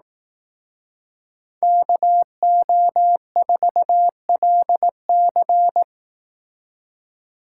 Morse code, heard as a keyed tone, about 18 wpm, 700 Hz. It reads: KO4LC
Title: Speed Racing - USA 2x2 Call Signs 15wpm